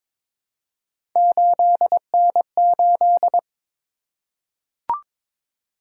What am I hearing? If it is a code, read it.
Morse code, heard as a keyed tone, about 22 wpm, 700 Hz. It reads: 8N8
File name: practice / 22wpm / Three Letter-Number Combinations - 22wpm